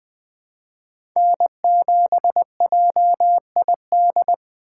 Morse code, heard as a keyed tone, about 20 wpm, 700 Hz. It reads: N7JID
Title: USA Call Signs - 2x Character Spacing - 20wpm